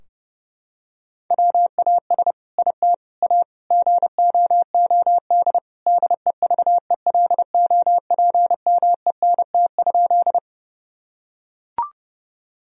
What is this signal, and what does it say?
Morse code, heard as a keyed tone, about 32 wpm, 700 Hz. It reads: WAS IT A GOOD DEVELOPMENT?
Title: Sentences from Top 1000 Words - Review Part 4 - 30wpm